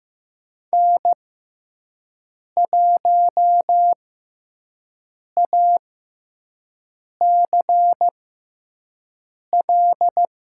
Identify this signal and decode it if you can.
Morse code, heard as a keyed tone, about 15 wpm, spaced as for 5 wpm, 700 Hz. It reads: N1ACL
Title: USA Call Signs - 5 Characters - 6x Character Spacing - 15wpm